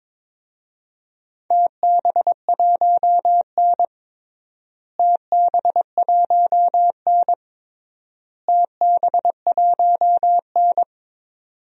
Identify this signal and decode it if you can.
Morse code, heard as a keyed tone, about 22 wpm, 700 Hz. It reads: TB1N TB1N TB1N
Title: DXCC-100 Call Signs - 4 Characters - Repeated 3x - 22wpm